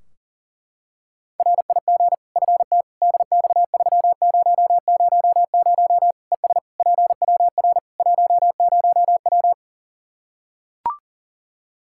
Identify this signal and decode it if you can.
Morse code, heard as a keyed tone, about 40 wpm, 700 Hz. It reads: RIG FT DX3000 ES PWR 10W
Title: Practice QSO - Exchange 2 - Info - 40wpm